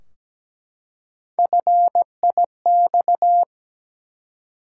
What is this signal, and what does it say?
Morse code, heard as a keyed tone, about 17 wpm, 700 Hz. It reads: FIX
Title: Three Letter Words - 17wpm